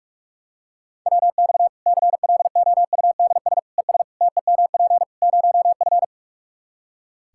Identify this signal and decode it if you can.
Morse code, heard as a keyed tone, about 46 wpm, 700 Hz. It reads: WX CLOUDS ES TEMP 0F
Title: Practice QSO - Exchange 2 - Info - 45wpm